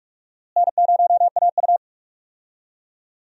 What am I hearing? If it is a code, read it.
Morse code, heard as a keyed tone, about 46 wpm, 700 Hz. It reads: N0AU
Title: USA Call Signs - 4 Characters - Repeated 3x - 45wpm